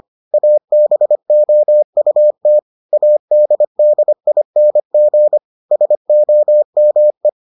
Morse code, heard as a keyed tone, about 25 wpm, 600 Hz. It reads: ABOUT ADDING SOME